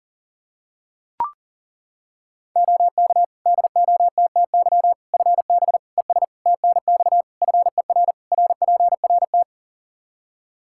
Morse code, heard as a keyed tone, about 37 wpm, 700 Hz. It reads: OK DOTTY FB ES TNX FER RPRT
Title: K1USN SST QSO Practice 40wpm